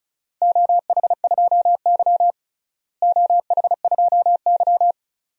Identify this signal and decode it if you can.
Morse code, heard as a keyed tone, about 36 wpm, 700 Hz. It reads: OH2Y OH2Y
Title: DXCC-100 Call Signs - 4 Characters - Repeated 2x - 35wpm